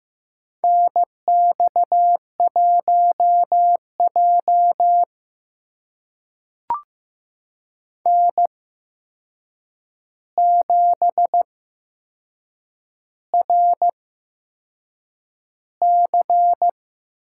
Morse code, heard as a keyed tone, about 15 wpm, spaced as for 6 wpm, 700 Hz. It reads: NX1J N7RC
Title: USA Call Signs - 4 Characters - 8x Character Spacing - 15wpm